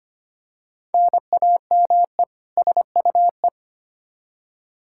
Morse code, heard as a keyed tone, about 25 wpm, 700 Hz. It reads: NAME SUE
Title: Practice QSO - Exchange 1 - Beginning - 25wpm